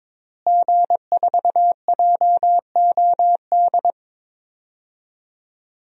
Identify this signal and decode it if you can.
Morse code, heard as a keyed tone, about 22 wpm, 700 Hz. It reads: G4JOD
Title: DXCC-100 Call Signs - 5 Characters - Repeated 3x - 22wpm